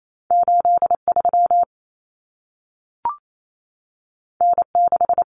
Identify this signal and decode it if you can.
Morse code, heard as a keyed tone, about 28 wpm, 700 Hz. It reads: 83 N6
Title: Common Cut Numbers - 2-Digit 28wpm